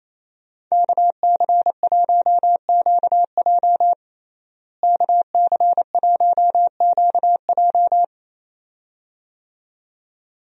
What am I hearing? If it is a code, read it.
Morse code, heard as a keyed tone, about 28 wpm, 700 Hz. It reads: KC1QJ KC1QJ
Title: USA Call Signs - 5 Characters - Repeated 2x - 28wpm